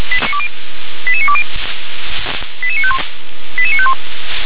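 Five-tone selective call (ZVEI), heard as quick runs of five tones, tones of about 70 ms. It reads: 88920 88951 88951
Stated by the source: ZVEI five-tone call 88920 received with an SDR.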